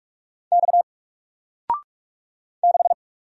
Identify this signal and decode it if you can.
Morse code, heard as a keyed tone, about 44 wpm, 700 Hz. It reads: X 6
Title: Single Letter-Number Alt 45wpm